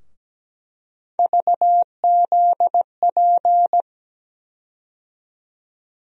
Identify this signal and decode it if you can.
Morse code, heard as a keyed tone, about 17 wpm, 700 Hz. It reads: VZP
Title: USA Call Sign Postfixes - 3 Letters 17wpm